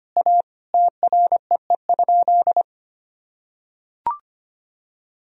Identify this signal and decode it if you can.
Morse code, heard as a keyed tone, about 25 wpm, 700 Hz. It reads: A TREE?
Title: Sentences from Top 1000 Words - Review Part 2 - 25wpm